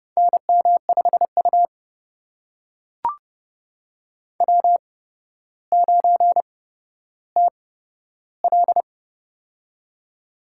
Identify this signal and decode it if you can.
Morse code, heard as a keyed tone, about 30 wpm, spaced as for 13 wpm, 700 Hz. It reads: NM5U W9TL
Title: USA Call Signs - 4 Characters - 8x Character Spacing - 30wpm